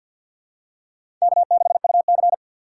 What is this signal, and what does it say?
Morse code, heard as a keyed tone, about 50 wpm, 700 Hz. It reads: K6UC
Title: USA Call Signs - 4 Characters - 6x Character Spacing - 50wpm